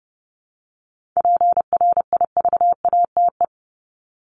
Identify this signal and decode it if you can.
Morse code, heard as a keyed tone, about 30 wpm, 700 Hz. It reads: PRIVATE
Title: Top 751-1000 Words 30wpm